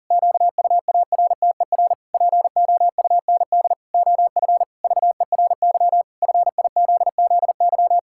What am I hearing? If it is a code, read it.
Morse code, heard as a keyed tone, about 37 wpm, 700 Hz. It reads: QUARTER POUND OF VERY FIZZY
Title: Pangram Challenge v2 - 40wpm